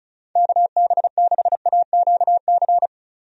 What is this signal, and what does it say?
Morse code, heard as a keyed tone, about 34 wpm, 700 Hz. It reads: KB6AQC
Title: USA Call Signs - 6 Characters - 35wpm